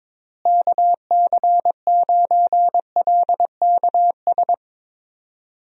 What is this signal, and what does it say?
Morse code, heard as a keyed tone, about 22 wpm, 700 Hz. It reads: KC9LKS
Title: USA Call Signs - 6 Characters - Repeated 3x - 22wpm